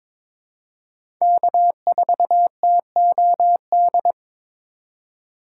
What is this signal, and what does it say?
Morse code, heard as a keyed tone, about 21 wpm, 700 Hz. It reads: K4TOD
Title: USA Call Signs - 5 Characters - 6x Character Spacing - 22wpm